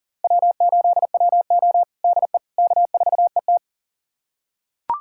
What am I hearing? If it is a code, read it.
Morse code, heard as a keyed tone, about 40 wpm, 700 Hz. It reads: W8WO DE K4ET
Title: USA Callsign DE Callsign - 4 Characters - 40wpm